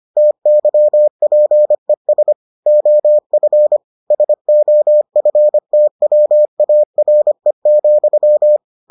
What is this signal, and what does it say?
Morse code, heard as a keyed tone, about 25 wpm, 600 Hz. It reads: TYPES OF SOFTWARE,